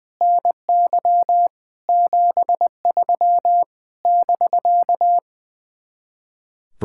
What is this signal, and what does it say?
Morse code, heard as a keyed tone, about 20 wpm, 700 Hz. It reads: NY 73 <BK>
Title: Speed-Racing Common QSO Phrases Encore - 20wpm